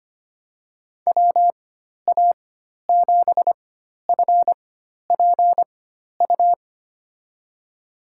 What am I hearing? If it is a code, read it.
Morse code, heard as a keyed tone, about 25 wpm, spaced as for 12 wpm, 700 Hz. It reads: WA7FPU
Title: USA Call Signs - 6 Characters - 4x Character Spacing - 25wpm